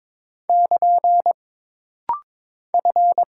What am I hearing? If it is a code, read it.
Morse code, heard as a keyed tone, about 22 wpm, 700 Hz. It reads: <KN> F
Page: This rapid-fire timing is geared to push your ICR (Instant Character Recognition) skills.